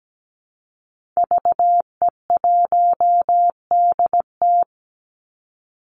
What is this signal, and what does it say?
Morse code, heard as a keyed tone, about 17 wpm, 700 Hz. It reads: VE1DT
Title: Call Signs Repeated 3x - 17wpm